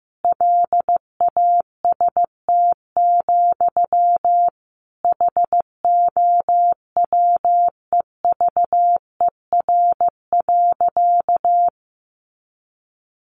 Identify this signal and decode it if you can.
Morse code, heard as a keyed tone, about 14 wpm, 700 Hz. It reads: LAST, HOWEVER.
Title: Sentences from Top 600 Words 15wpm